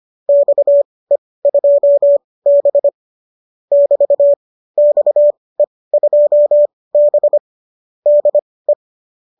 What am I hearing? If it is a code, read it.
Morse code, heard as a keyed tone, about 25 wpm, spaced as for 18 wpm, 590 Hz. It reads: XE2B <BT> XE2B DE
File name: qso204 18